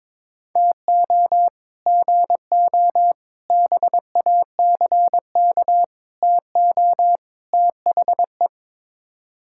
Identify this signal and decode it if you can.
Morse code, heard as a keyed tone, about 22 wpm, 700 Hz. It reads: TO GO BACK TO THE
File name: practice / 22wpm / Build A Phrase - 5-Word Phrases - 22wpm